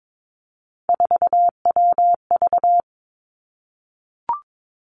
Morse code, heard as a keyed tone, about 22 wpm, 700 Hz. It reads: 4WV
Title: Three Challenging Letter-Number Combinations 22wpm